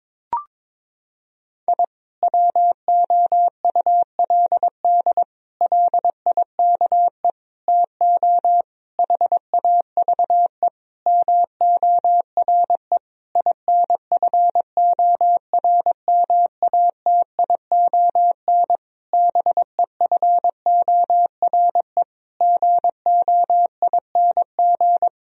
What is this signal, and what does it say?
Morse code, heard as a keyed tone, about 22 wpm, 700 Hz. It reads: I WOULD LIKE TO HAVE MORE INFORMATION BEFORE GOING
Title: Sentences from Top 1000 Words - Review Part 3 - 22wpm